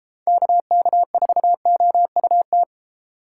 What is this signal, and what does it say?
Morse code, heard as a keyed tone, about 32 wpm, 700 Hz. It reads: KK4OUT